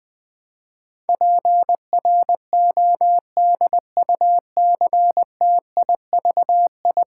Morse code, heard as a keyed tone, about 20 wpm, 700 Hz. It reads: PRODUCTIVI
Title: Build A Word - 5 Letters At A Time - 12-Letter Words - 20wpm